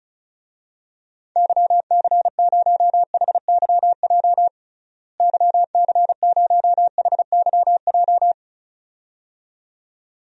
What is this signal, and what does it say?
Morse code, heard as a keyed tone, about 34 wpm, 700 Hz. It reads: YC0HYJ YC0HYJ
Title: DXCC-100 Call Signs - 6 Characters - Repeated 2x - 35wpm